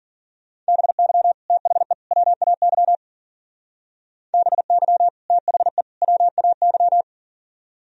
Morse code, heard as a keyed tone, about 48 wpm, 700 Hz. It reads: BY THE WAY BY THE WAY
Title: Speed-Racing - Sets of 3 Words 40wpm